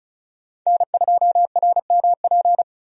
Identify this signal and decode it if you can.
Morse code, heard as a keyed tone, about 36 wpm, 700 Hz. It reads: N2RMP
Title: USA Call Signs - 5 Characters - 35wpm